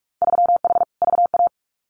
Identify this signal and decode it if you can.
Morse code, heard as a keyed tone, about 46 wpm, 700 Hz. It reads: 3H VA
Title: Winter Field Day Exchanges 45wpm